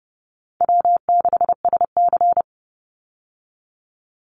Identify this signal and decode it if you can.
Morse code, heard as a keyed tone, about 30 wpm, 700 Hz. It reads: W6SC
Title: Easy Call Signs Repeated 3x - 30wpm